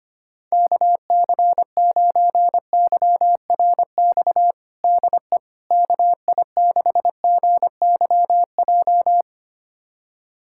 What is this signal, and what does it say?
Morse code, heard as a keyed tone, about 25 wpm, 700 Hz. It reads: KC9YRX DE KI6GYJ
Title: USA Callsign DE Callsign 25wpm